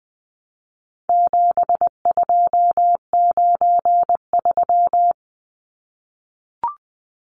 Morse code, heard as a keyed tone, about 20 wpm, 700 Hz. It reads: 7293
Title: Numbers - 4-Digit 20wpm